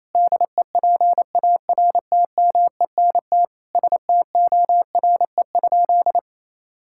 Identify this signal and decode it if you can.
Morse code, heard as a keyed tone, about 28 wpm, 700 Hz. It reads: DEPARTMENT STORE?
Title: Sentences from Top 1000 Words - Review Part 5 - 28wpm